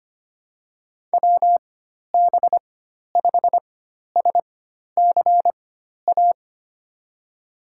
Morse code, heard as a keyed tone, about 25 wpm, spaced as for 12 wpm, 700 Hz. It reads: WB5SCA
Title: USA Call Signs - 6 Characters - 4x Character Spacing - 25wpm